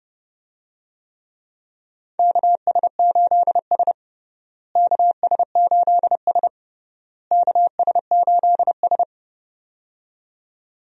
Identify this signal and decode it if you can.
Morse code, heard as a keyed tone, about 32 wpm, 700 Hz. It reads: KS8S KS8S KS8S
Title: USA Call Signs - 4 Characters - Repeated 3x - 30wpm